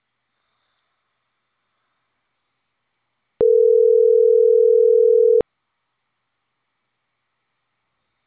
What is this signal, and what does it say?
Ringback tone (North America), 440 + 480 Hz.